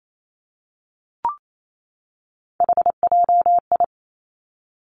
Morse code, heard as a keyed tone, about 28 wpm, 700 Hz. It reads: HJI
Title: Three Challenging Letter-Number Combinations 28wpm